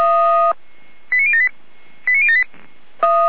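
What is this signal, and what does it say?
Five-tone selective call (ZVEI), heard as quick runs of five tones, tones of about 70 ms. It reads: 89078 89078
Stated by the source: ZVEI five-tone selective call 89160, received on an SDR, from the Feuerwehr Orlamünde fire brigade audio server.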